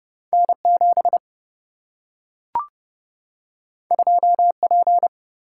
Morse code, heard as a keyed tone, about 30 wpm, 700 Hz. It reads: N 7 2 P
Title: Single-Letter Single-Number 30wpm